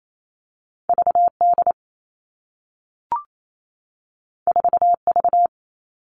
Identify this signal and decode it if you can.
Morse code, heard as a keyed tone, about 28 wpm, 700 Hz. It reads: VD 4V
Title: All Cut Numbers - 2-Digit 28wpm